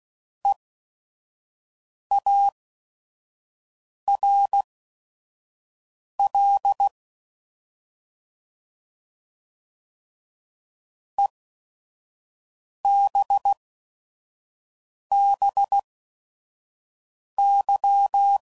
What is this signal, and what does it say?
Morse code, heard as a keyed tone, about 16 wpm, spaced as for 5 wpm, 800 Hz. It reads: EARL EBBY